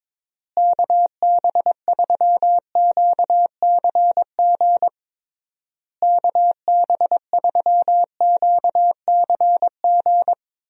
Morse code, heard as a keyed tone, about 22 wpm, 700 Hz. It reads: KB3QCG KB3QCG
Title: USA Call Signs - 6 Characters - Repeated 3x - 22wpm